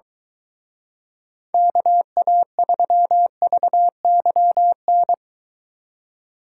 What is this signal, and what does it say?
Morse code, heard as a keyed tone, about 22 wpm, 700 Hz. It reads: KA3VYN